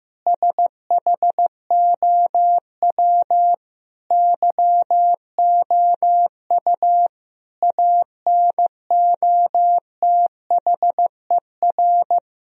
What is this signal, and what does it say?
Morse code, heard as a keyed tone, about 15 wpm, 700 Hz. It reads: SHOW YOU ANOTHER
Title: Sentences from Top 1000 Words - Review Part 3 - 15wpm